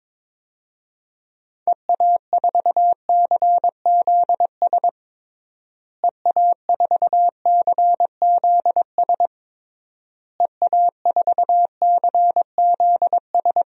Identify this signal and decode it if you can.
Morse code, heard as a keyed tone, about 21 wpm, 700 Hz. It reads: EA4CZS EA4CZS EA4CZS
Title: DXCC-100 Call Signs - 6 Characters - Repeated 3x - 22wpm